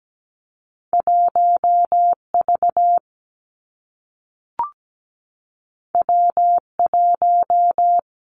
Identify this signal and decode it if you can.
Morse code, heard as a keyed tone, about 17 wpm, 700 Hz. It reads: 1V W1
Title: Two Challenging Letter-Number Combinations 17wpm